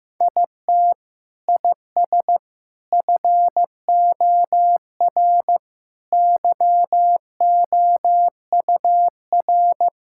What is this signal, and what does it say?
Morse code, heard as a keyed tone, about 16 wpm, 700 Hz. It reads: IT IS FOR YOUR
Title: Sentences from Top 1600 Words 15wpm